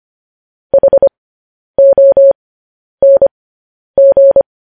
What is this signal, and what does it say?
Morse code, heard as a keyed tone, about 25 wpm, spaced as for 10 wpm, 550 Hz. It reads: HONG
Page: news headlines in morse code at 10 wpm